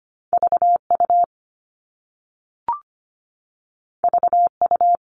Morse code, heard as a keyed tone, about 25 wpm, 700 Hz. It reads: V U V U
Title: All Cut Numbers - 2-Digit 25wpm